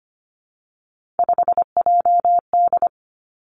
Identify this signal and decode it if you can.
Morse code, heard as a keyed tone, about 25 wpm, 700 Hz. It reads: 5JD
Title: Three Challenging Letter-Number Combinations 25wpm